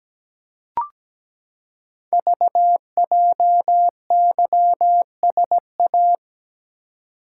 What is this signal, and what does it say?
Morse code, heard as a keyed tone, about 17 wpm, 700 Hz. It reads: VJYSA
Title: Code Groups - 5 Letters - 17wpm